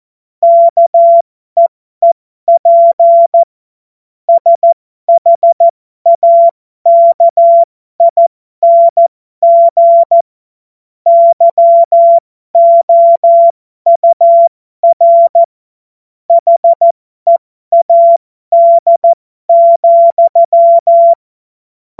Morse code, heard as a keyed tone, about 14 wpm, spaced as for 12 wpm, 680 Hz. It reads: KEEP SHAKING YOUR HEAD,
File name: Morsekurs / Witze